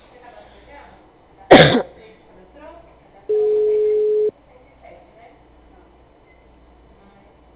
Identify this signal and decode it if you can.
Ringback tone (Europe, 425 Hz), with speech.
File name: out- -2019- - - .21670